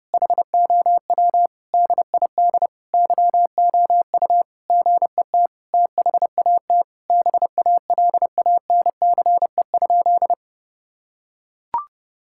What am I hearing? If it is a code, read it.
Morse code, heard as a keyed tone, about 30 wpm, 700 Hz. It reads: HOW DID YOU GET THAT BALANCE?
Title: Sentences from Top 1000 Words - Review Part 6 - 30wpm